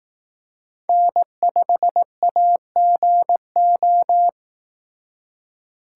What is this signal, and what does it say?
Morse code, heard as a keyed tone, about 18 wpm, 700 Hz. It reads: N5AGO
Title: Speed Racing - USA 1x3 Call Signs 15wpm